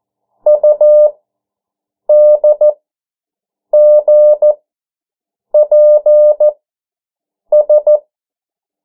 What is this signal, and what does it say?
Morse code, heard as a keyed tone, about 14 wpm, spaced as for 7 wpm, 600 Hz. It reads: UDGPS
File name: morse mp3 / 09 urok 8